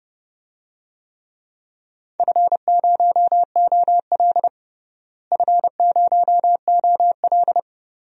Morse code, heard as a keyed tone, about 30 wpm, 700 Hz. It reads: F0OL F0OL
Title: DXCC-100 Call Signs - 4 Characters - Repeated 2x - 30wpm